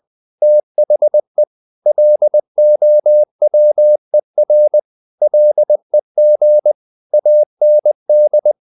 Morse code, heard as a keyed tone, about 20 wpm, 600 Hz. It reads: THE LOWER LEG AND
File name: the old man and the sea part 2 20wpm